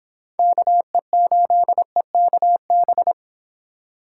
Morse code, heard as a keyed tone, about 26 wpm, 700 Hz. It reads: KE8EKB